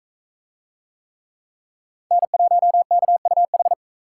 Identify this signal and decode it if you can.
Morse code, heard as a keyed tone, about 44 wpm, 700 Hz. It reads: N1KUH